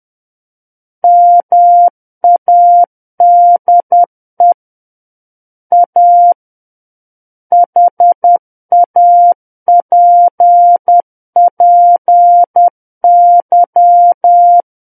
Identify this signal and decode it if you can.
Morse code, heard as a keyed tone, about 10 wpm, 700 Hz. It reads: MADE A HAPPY
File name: Book One / 01-Chapter 1 - 10 WPM